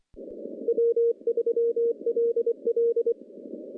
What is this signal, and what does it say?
Morse code, heard as a keyed tone, about 26 wpm, 470 Hz. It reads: W3LL
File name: contests / CW-SS-k5tr- -1300